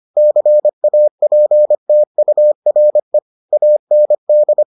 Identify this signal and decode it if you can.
Morse code, heard as a keyed tone, about 25 wpm, 600 Hz. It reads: CAPTURE AND